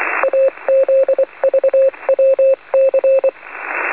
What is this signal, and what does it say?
Morse code, heard as a keyed tone, about 24 wpm, 540 Hz, with radio noise. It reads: AZVWC
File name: 4.440-CW-in-USB-niet-mee-te-lezen-15-06-2012-1